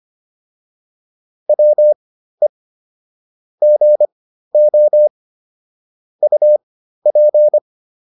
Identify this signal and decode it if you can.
Morse code, heard as a keyed tone, about 25 wpm, spaced as for 13 wpm, 600 Hz. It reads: WE GO UP